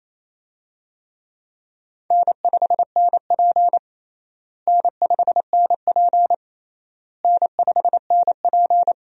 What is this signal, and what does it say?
Morse code, heard as a keyed tone, about 28 wpm, 700 Hz. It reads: N5NP N5NP N5NP
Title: USA Call Signs - 4 Characters - Repeated 3x - 28wpm